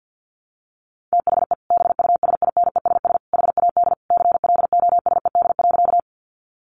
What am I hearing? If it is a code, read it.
Morse code, heard as a keyed tone, about 50 wpm, 700 Hz. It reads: THE BUSINESS HAD CLOSED.
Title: Sentences from Top 1400 Words 50wpm